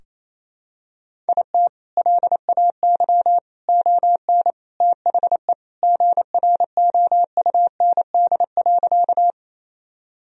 Morse code, heard as a keyed tone, about 28 wpm, 700 Hz. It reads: IT LAY ON THE GROUND.
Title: Sentences from Top 1000 Words - Review Part 2 - 28wpm